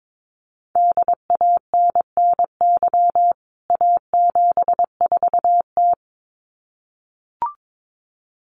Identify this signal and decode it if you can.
Morse code, heard as a keyed tone, about 22 wpm, 700 Hz. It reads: DANNY A74T
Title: CWT Exchange 22wpm